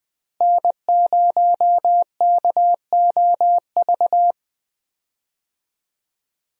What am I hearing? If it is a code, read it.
Morse code, heard as a keyed tone, about 20 wpm, 700 Hz. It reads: N0KOV
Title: USA Call Signs - 5 Characters - Repeated 3x - 20wpm